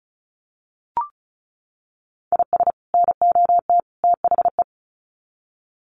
Morse code, heard as a keyed tone, about 34 wpm, 700 Hz. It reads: IS NOT THE
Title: Sets of 3 Words 35wpm